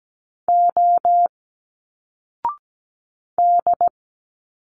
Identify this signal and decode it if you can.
Morse code, heard as a keyed tone, about 17 wpm, 700 Hz. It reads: O D
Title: Single Letters 17wpm